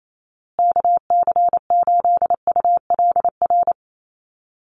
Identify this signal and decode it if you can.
Morse code, heard as a keyed tone, about 28 wpm, 700 Hz. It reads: KC8ULR
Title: USA Call Signs Repeated 2x 28wpm